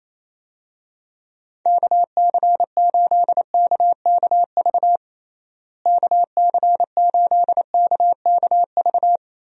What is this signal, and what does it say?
Morse code, heard as a keyed tone, about 28 wpm, 700 Hz. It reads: KC8KKV KC8KKV
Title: USA Call Signs - 6 Characters - Repeated 2x - 28wpm